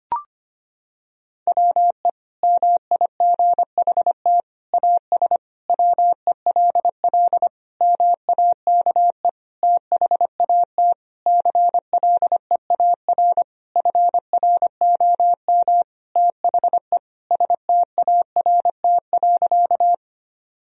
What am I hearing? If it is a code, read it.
Morse code, heard as a keyed tone, about 25 wpm, 700 Hz. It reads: WE MIGHT AS WELL MAKE THAT CLEAR FROM THE START.
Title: Sentences from Top 1000 Words - Review Part 1 - 25wpm